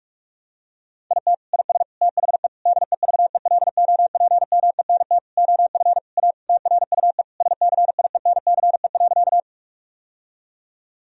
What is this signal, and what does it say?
Morse code, heard as a keyed tone, about 46 wpm, 700 Hz. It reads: IT IS THE DEVELOPMENT OF A TRUE SCIENCE.
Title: Sentences from Top 1000 Words - Review Part 4 - 45wpm